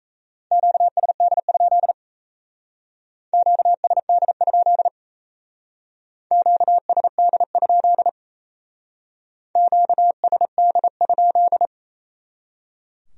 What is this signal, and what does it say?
Morse code, heard as a keyed tone, about 35 wpm, 700 Hz. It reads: QSD? QSD? QSD? QSD?
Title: Speed-Racing Q-Codes - 28wpm